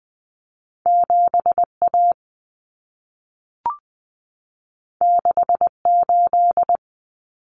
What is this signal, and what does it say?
Morse code, heard as a keyed tone, about 20 wpm, 700 Hz. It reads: 7A 68
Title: Common Cut Numbers - 2-Digit 20wpm